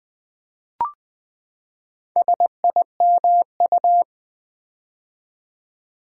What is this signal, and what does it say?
Morse code, heard as a keyed tone, about 20 wpm, 700 Hz. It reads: SIMU
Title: Build A Word - 4 Letters At A Time - 8-Letter Words - 20wpm